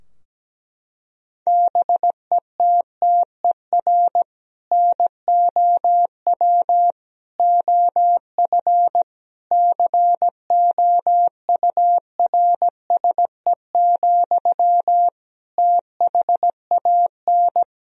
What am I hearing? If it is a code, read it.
Morse code, heard as a keyed tone, about 17 wpm, 700 Hz. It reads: BETTER NOW OF COURSE, THAN
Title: Sentences from Top 1000 Words - Review Part 3 - 17wpm